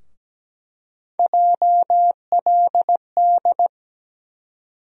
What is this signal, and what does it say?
Morse code, heard as a keyed tone, about 17 wpm, 700 Hz. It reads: JLD
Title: USA Call Sign Postfixes - 3 Letters 17wpm